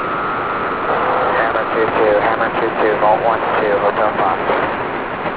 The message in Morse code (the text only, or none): none